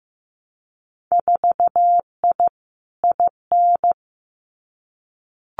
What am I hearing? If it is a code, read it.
Morse code, heard as a keyed tone, about 15 wpm, 700 Hz. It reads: 4I IN
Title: Winter Field Day Exchanges 15wpm